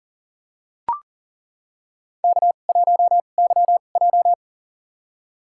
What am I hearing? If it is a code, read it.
Morse code, heard as a keyed tone, about 40 wpm, spaced as for 29 wpm, 700 Hz. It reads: K1YJ
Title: USA Call Signs - 2x Character Spacing - 40wpm